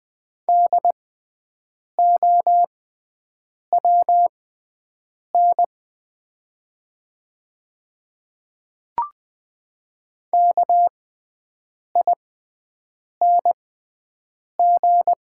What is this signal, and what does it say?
Morse code, heard as a keyed tone, about 20 wpm, spaced as for 7 wpm, 700 Hz. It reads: DOWN KING
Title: 6x Character Spacing - Top 500 Words - 20wpm